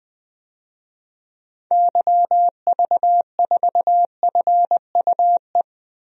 Morse code, heard as a keyed tone, about 20 wpm, 700 Hz. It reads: YV4FUE
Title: DXCC-100 Call Signs - 6 Characters - Repeated 2x - 20wpm